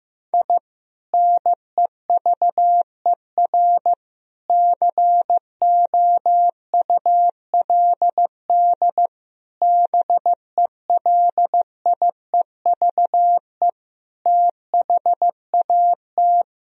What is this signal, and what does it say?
Morse code, heard as a keyed tone, about 16 wpm, 700 Hz. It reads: I NEVER COULD BELIEVE THAT
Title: Sentences from Top 1000 Words - Review Part 1 - 15wpm